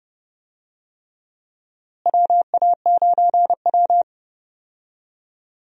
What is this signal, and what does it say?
Morse code, heard as a keyed tone, about 31 wpm, 700 Hz. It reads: WA9W
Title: USA Call Signs - 4 Characters - Repeated 3x - 30wpm